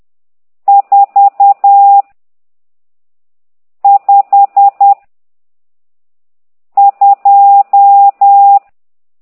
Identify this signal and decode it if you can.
Morse code, heard as a keyed tone, about 10 wpm, 800 Hz. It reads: 4 5 2
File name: ct2idw / exerc12